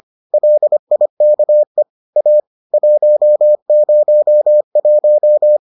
Morse code, heard as a keyed tone, about 25 wpm, 600 Hz. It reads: LIKE A 101